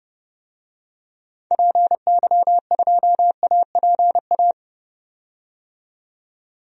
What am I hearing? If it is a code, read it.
Morse code, heard as a keyed tone, about 30 wpm, 700 Hz. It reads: PY2APA
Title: DXCC-100 Call Signs - 6 Characters - 30wpm